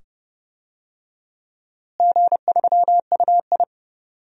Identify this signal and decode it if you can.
Morse code, heard as a keyed tone, about 30 wpm, 700 Hz. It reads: G3UI
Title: DXCC-100 Call Signs - 4 Characters - Repeated 2x - 30wpm